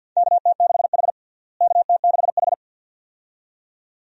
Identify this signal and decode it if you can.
Morse code, heard as a keyed tone, about 46 wpm, 700 Hz. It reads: KT6H KT6H
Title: USA Call Signs - 4 Characters - Repeated 2x - 50wpm